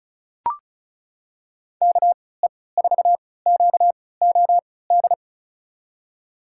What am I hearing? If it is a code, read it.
Morse code, heard as a keyed tone, about 36 wpm, spaced as for 20 wpm, 700 Hz. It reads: KE4QOD
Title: USA Call Signs - 3x Character Spacing - 35wpm